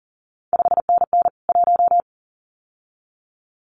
Morse code, heard as a keyed tone, about 38 wpm, 700 Hz. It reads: 5NN 1
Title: CQ WW DX Contest Exchange Encore - 40wpm